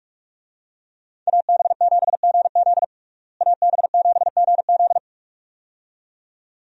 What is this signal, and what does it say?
Morse code, heard as a keyed tone, about 46 wpm, 700 Hz. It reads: AB7GZ AB7GZ
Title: USA Call Signs - 5 Characters - Repeated 2x - 45wpm